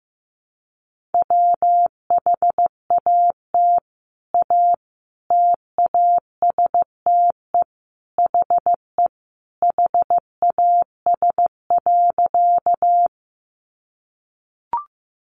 Morse code, heard as a keyed tone, about 15 wpm, 700 Hz. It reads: WHAT A TASTE HE HAS.
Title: Sentences from Top 1100 Words 15wpm